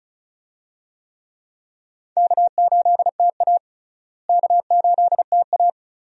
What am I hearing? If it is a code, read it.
Morse code, heard as a keyed tone, about 36 wpm, 700 Hz. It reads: K8TA K8TA
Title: USA Call Signs - 4 Characters - Repeated 2x - 35wpm